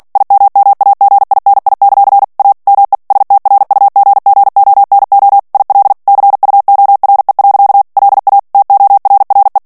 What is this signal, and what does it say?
Morse code, heard as a keyed tone, about 48 wpm, 800 Hz. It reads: IMMAGINI; A ME STRUGGONO IL CUORE. LA TORRE
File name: morse mp3 / ITALIEN texte clair 1 vitesse 45 mots minute